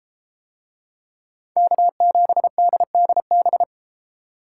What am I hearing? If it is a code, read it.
Morse code, heard as a keyed tone, about 33 wpm, 700 Hz. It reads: K7DDB